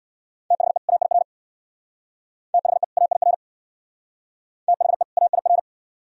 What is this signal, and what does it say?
Morse code, heard as a keyed tone, about 62 wpm, 700 Hz. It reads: THE AIR THE AIR THE AIR
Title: Speed-Racing - Sets of 2 Words 50wpm